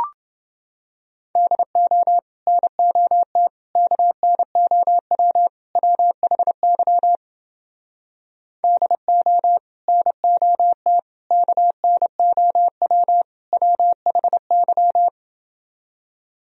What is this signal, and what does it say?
Morse code, heard as a keyed tone, about 30 wpm, 700 Hz. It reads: DO NOT KNOW WHY DO NOT KNOW WHY